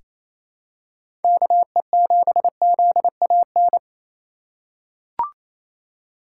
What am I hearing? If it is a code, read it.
Morse code, heard as a keyed tone, about 28 wpm, 700 Hz. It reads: KE7ZAN
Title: USA Call Signs - 6 Characters - 2x Character Spacing - 28wpm